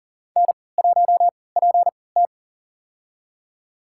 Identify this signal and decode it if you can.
Morse code, heard as a keyed tone, about 40 wpm, spaced as for 23 wpm, 700 Hz. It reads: N1PT
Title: USA Call Signs - 4 Characters - 3x Character Spacing - 40wpm